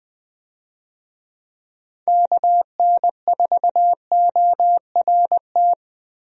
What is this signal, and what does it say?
Morse code, heard as a keyed tone, about 21 wpm, 700 Hz. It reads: KN4ORT
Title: USA Call Signs - 6 Characters - Repeated 2x - 20wpm